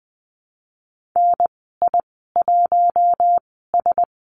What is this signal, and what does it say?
Morse code, heard as a keyed tone, about 20 wpm, spaced as for 15 wpm, 700 Hz. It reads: NI1S
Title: 055 Lesson 10g - Letter r - Call Signs 20wpm